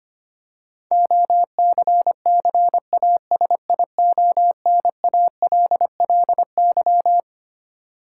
Morse code, heard as a keyed tone, about 25 wpm, 700 Hz. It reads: OCCASIONALLY
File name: Twelve Letter Words - 25wpm